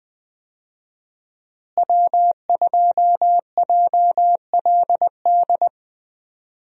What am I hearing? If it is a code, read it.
Morse code, heard as a keyed tone, about 20 wpm, 700 Hz. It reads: W2JLD
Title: USA Call Signs - 5 Characters - Repeated 2x - 20wpm